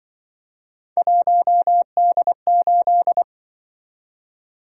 Morse code, heard as a keyed tone, about 24 wpm, 700 Hz. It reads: 1D8
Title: Speed-Racing - Three Letter-Number Combinations - 20wpm